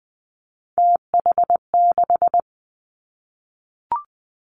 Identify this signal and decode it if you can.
Morse code, heard as a keyed tone, about 20 wpm, 700 Hz. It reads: TH6
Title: Three Challenging Letter-Number Combinations 20wpm